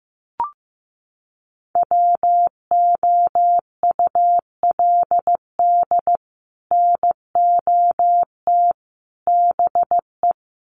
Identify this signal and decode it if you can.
Morse code, heard as a keyed tone, about 15 wpm, 700 Hz. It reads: WOULD NOT BE
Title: Sets of 3 Words 15wpm